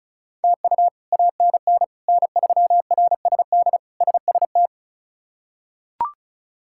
Morse code, heard as a keyed tone, about 36 wpm, 700 Hz. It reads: TU ANN N3RSD SST
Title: K1USN SST QSO Practice - 35wpm